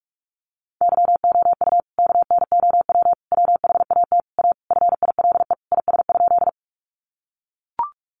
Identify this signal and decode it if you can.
Morse code, heard as a keyed tone, about 44 wpm, 700 Hz. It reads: YOU KNOW WHAT A FILE IS?
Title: Sentences from Top 1200 Words 45wpm